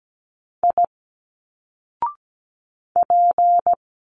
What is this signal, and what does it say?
Morse code, heard as a keyed tone, about 17 wpm, 700 Hz. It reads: I P
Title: Single Letters 17wpm